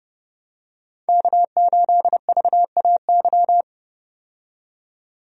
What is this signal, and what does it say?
Morse code, heard as a keyed tone, about 32 wpm, 700 Hz. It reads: K8VAY
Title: USA Call Signs - 5 Characters - 30wpm